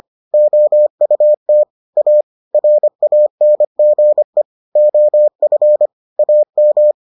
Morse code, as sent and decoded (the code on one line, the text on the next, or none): --- ..- - / .- / .-. .- -. --. / --- ..-. / .- --
OUT A RANGE OF AM